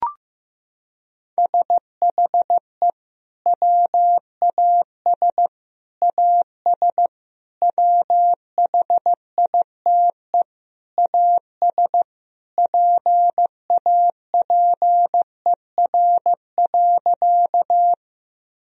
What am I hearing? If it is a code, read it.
Morse code, heard as a keyed tone, about 15 wpm, 700 Hz. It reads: SHE WAS AS WHITE AS PAPER.
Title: Sentences from Top 1000 Words - Review Part 1 - 15wpm